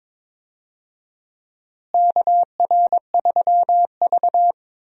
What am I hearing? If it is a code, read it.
Morse code, heard as a keyed tone, about 22 wpm, 700 Hz. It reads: KR3V
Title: USA Call Signs - 4 Characters - Repeated 2x - 22wpm